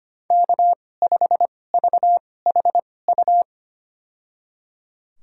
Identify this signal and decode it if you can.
Morse code, heard as a keyed tone, about 25 wpm, spaced as for 18 wpm, 700 Hz. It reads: K5VHU
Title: USA Call Signs - 2x Character Spacing - 25wpm